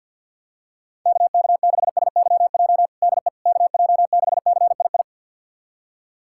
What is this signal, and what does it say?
Morse code, heard as a keyed tone, about 53 wpm, 700 Hz. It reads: KK6SYJ DE KJ6CII
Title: USA Callsign DE Callsign - 6 Characters - 50wpm